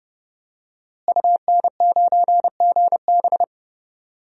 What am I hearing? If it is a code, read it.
Morse code, heard as a keyed tone, about 31 wpm, 700 Hz. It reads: UN9GB
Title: DXCC-100 Call Signs - 5 Characters - Repeated 2x - 30wpm